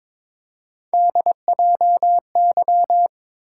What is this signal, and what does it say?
Morse code, heard as a keyed tone, about 22 wpm, 700 Hz. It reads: DJY
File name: practice / 22wpm / Three Letter-Number Combinations - 22wpm